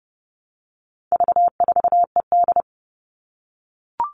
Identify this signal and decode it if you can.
Morse code, heard as a keyed tone, about 30 wpm, 700 Hz. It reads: V4ED
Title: All Cut Numbers - 4-Digit 30wpm